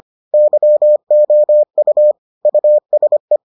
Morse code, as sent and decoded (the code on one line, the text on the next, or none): -.-- --- ..- / ..- ... .
YOU USE